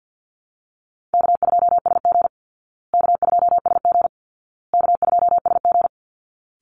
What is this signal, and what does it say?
Morse code, heard as a keyed tone, about 48 wpm, 700 Hz. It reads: K2SG K2SG K2SG
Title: Easy Call Signs Repeated 3x - 50wpm